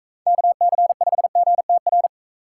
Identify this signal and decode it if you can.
Morse code, heard as a keyed tone, about 40 wpm, 700 Hz. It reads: KC5GTR